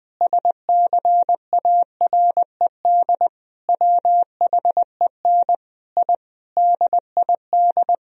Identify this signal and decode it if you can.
Morse code, heard as a keyed tone, about 20 wpm, 700 Hz. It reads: SCARED WHEN I DID
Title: Sentences from Top 1000 Words - Review Part 4 - 20wpm